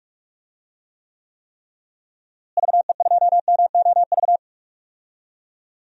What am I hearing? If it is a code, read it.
Morse code, heard as a keyed tone, about 46 wpm, 700 Hz. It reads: VE2MOV
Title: DXCC-100 Call Signs - 6 Characters - Repeated 3x - 45wpm